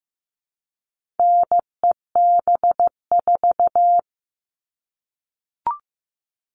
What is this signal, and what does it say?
Morse code, heard as a keyed tone, about 15 wpm, 700 Hz. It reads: NEB4
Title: All Cut Numbers - 4-Digit 15wpm